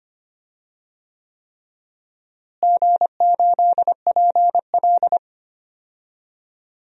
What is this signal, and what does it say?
Morse code, heard as a keyed tone, about 25 wpm, 700 Hz. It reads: G8PL
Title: DXCC-100 Call Signs - 4 Characters - Repeated 3x - 25wpm